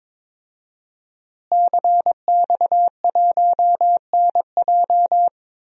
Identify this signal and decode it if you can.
Morse code, heard as a keyed tone, about 22 wpm, 700 Hz. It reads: CX1NJ
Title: DXCC-100 Call Signs - 5 Characters - Repeated 2x - 22wpm